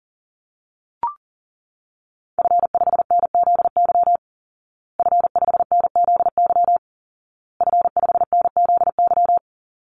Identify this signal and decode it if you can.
Morse code, heard as a keyed tone, about 40 wpm, 700 Hz. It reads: F5NZY F5NZY F5NZY
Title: Call Signs Repeated 2x - 40wpm